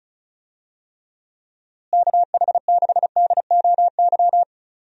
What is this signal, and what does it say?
Morse code, heard as a keyed tone, about 36 wpm, 700 Hz. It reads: KH6DOY
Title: DXCC-100 Call Signs - 6 Characters - Repeated 2x - 35wpm